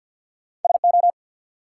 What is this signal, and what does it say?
Morse code, heard as a keyed tone, about 53 wpm, 700 Hz. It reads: SO
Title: Sentences from Top 100 Words - New Word Drill - 50wpm